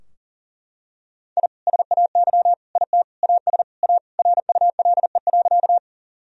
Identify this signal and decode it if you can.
Morse code, heard as a keyed tone, about 40 wpm, 700 Hz. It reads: I SAY IT AS A RULE.
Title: Sentences from Top 1000 Words - Review Part 2 - 40wpm